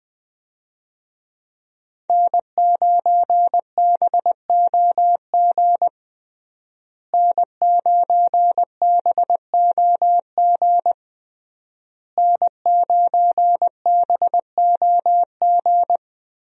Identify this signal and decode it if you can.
Morse code, heard as a keyed tone, about 20 wpm, 700 Hz. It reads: N9BOG N9BOG N9BOG
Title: USA Call Signs - 5 Characters - Repeated 3x - 20wpm